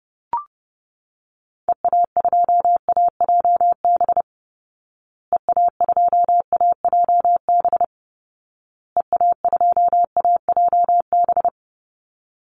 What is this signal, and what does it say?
Morse code, heard as a keyed tone, about 30 wpm, 700 Hz. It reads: EA2AJB EA2AJB EA2AJB
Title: Call Signs Repeated 2x - 30wpm